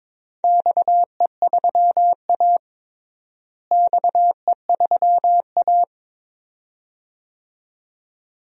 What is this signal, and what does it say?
Morse code, heard as a keyed tone, about 22 wpm, 700 Hz. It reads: XE3A XE3A
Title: DXCC-100 Call Signs - 4 Characters - Repeated 2x - 22wpm